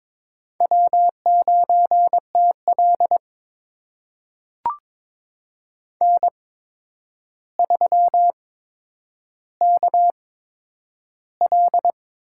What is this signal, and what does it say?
Morse code, heard as a keyed tone, about 23 wpm, spaced as for 10 wpm, 700 Hz. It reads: W9TL N3KL
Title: USA Call Signs - 4 Characters - 8x Character Spacing - 22wpm